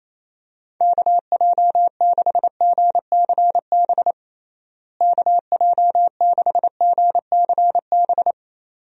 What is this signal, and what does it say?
Morse code, heard as a keyed tone, about 28 wpm, 700 Hz. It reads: KJ6GCB KJ6GCB
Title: USA Call Signs - 6 Characters - Repeated 2x - 28wpm